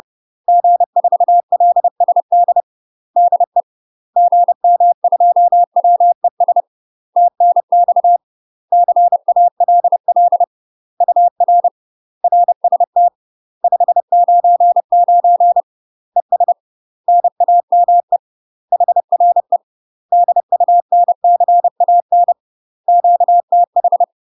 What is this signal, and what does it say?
Morse code, heard as a keyed tone, about 30 wpm, 700 Hz. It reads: G4LSD DE GM2WES TNX CALL UR RST 599 ES NAME HRE DUNCAN QTH
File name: UK MASTER QSOs TRAINING / 30WPM / UK MASTER QSOs 0021